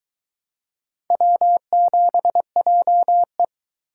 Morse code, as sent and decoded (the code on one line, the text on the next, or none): .-- --... .--- .
W7JE